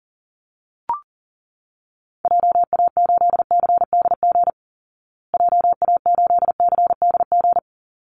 Morse code, heard as a keyed tone, about 40 wpm, 700 Hz. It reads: JA8CDG JA8CDG
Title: Call Signs Repeated 3x - 40wpm